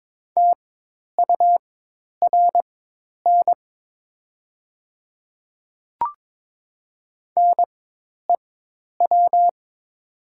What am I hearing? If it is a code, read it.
Morse code, heard as a keyed tone, about 21 wpm, spaced as for 10 wpm, 700 Hz. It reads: TURN NEW
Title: 4x Character Spacing - Top 200 Words - 22wpm